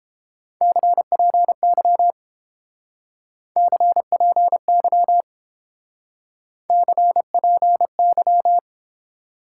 Morse code, heard as a keyed tone, about 30 wpm, 700 Hz. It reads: CPY CPY CPY